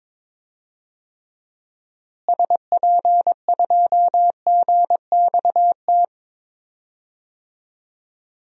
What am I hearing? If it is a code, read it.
Morse code, heard as a keyed tone, about 22 wpm, 700 Hz. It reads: SP2GXT
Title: DXCC-100 Call Signs - 6 Characters - Repeated 2x - 22wpm